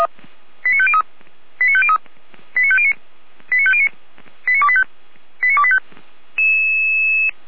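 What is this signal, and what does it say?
Five-tone selective call (ZVEI), heard as quick runs of five tones, tones of about 70 ms. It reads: 89503 89503 89509 89509 89285 89285
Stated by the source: ZVEI five-tone call 89220 received by SDR, from a Feuerwehr Orlamünde archive.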